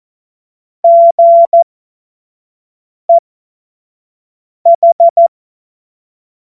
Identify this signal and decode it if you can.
Morse code, heard as a keyed tone, about 14 wpm, spaced as for 5 wpm, 680 Hz. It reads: GEH
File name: Morsekurs / Witze / Witz 27. April 2021, 25 F70